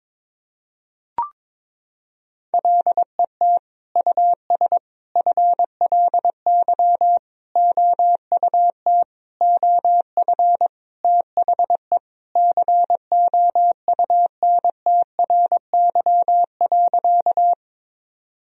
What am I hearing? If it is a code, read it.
Morse code, heard as a keyed tone, about 21 wpm, 700 Hz. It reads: LET US FLY OUT OF THE COUNTRY.
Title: Sentences from Top 1000 Words - Review Part 2 - 22wpm